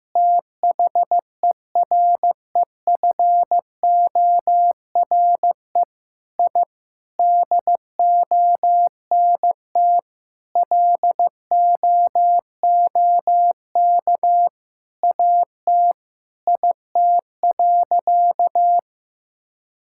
Morse code, heard as a keyed tone, about 16 wpm, 700 Hz. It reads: THEREFORE I DONT LOOK AT IT.
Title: Sentences from Top 1000 Words - Review Part 5 - 15wpm